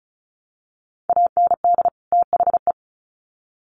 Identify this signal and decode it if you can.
Morse code, heard as a keyed tone, about 34 wpm, 700 Hz. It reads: AND THE
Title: Sets of 2 Words 35wpm